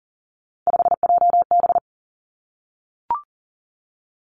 Morse code, heard as a keyed tone, about 43 wpm, 700 Hz. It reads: 5JB
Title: Three Challenging Letter-Number Combinations 40wpm